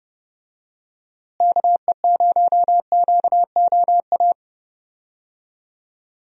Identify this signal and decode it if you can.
Morse code, heard as a keyed tone, about 30 wpm, 700 Hz. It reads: KE0QOA
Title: USA Call Signs - 6 Characters - Repeated 3x - 30wpm